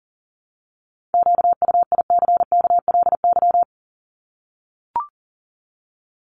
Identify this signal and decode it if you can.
Morse code, heard as a keyed tone, about 40 wpm, 700 Hz. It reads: QUICKLY
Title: Top 501-750 Words 40wpm